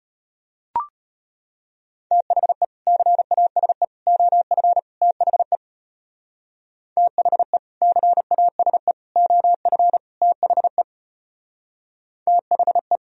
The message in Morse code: THE CASE OF THE THE CASE OF THE THE

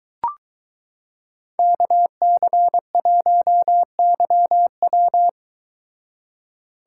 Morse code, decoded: KC1YW

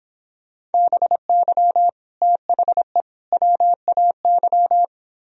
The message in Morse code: BY THE WAY

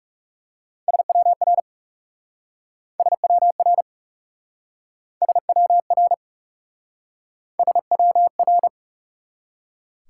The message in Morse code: SWR SWR SWR SWR

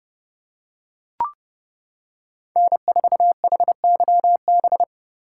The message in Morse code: N4HYB

